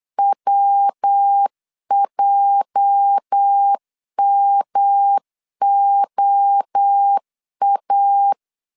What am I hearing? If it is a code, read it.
Morse code, heard as a keyed tone, about 8 wpm, 800 Hz. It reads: WJMOA
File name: LEZIONI / 8 YJ45 / 8 YJ 8wpm 3313